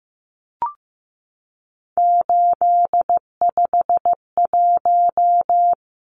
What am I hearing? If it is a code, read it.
Morse code, heard as a keyed tone, about 15 wpm, 700 Hz. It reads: 851
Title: Numbers - 3-Digit 15wpm